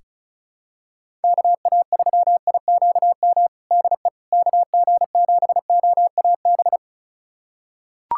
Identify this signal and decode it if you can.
Morse code, heard as a keyed tone, about 34 wpm, 700 Hz. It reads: KA3IQM DE KG7OAB
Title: USA Callsign DE Callsign - 6 Characters - 35wpm